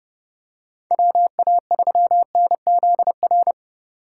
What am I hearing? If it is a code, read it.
Morse code, heard as a keyed tone, about 31 wpm, 700 Hz. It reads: WA3NZR
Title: K1USN SST QSO Practice - 30wpm